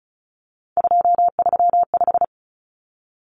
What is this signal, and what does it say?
Morse code, heard as a keyed tone, about 34 wpm, 700 Hz. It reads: 235
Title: Common Cut Numbers - 3-Digit 35wpm